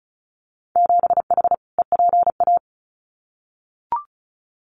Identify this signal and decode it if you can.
Morse code, heard as a keyed tone, about 36 wpm, 700 Hz. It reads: 7H EPA
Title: Winter Field Day Exchanges Encore 35wpm